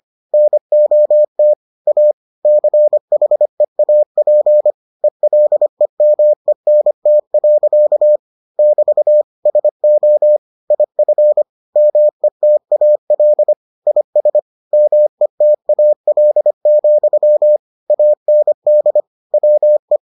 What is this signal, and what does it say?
Morse code, heard as a keyed tone, about 25 wpm, 600 Hz. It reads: NOT A CHEAP ELEMENT. <BT> SO IF METAL IS METAL, AND WE